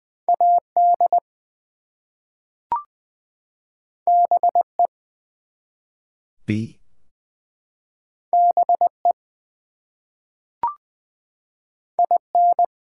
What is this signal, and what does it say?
Morse code, heard as a keyed tone, about 20 wpm, 700 Hz. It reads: AD BE BE IN
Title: Two Letter Words - 20wpm